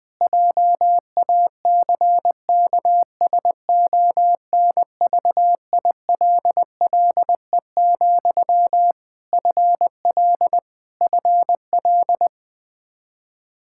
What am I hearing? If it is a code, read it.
Morse code, heard as a keyed tone, about 21 wpm, 700 Hz. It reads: JACKSONVILLE, FL FL
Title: Practice QSO - Exchange 1 - Beginning - 20wpm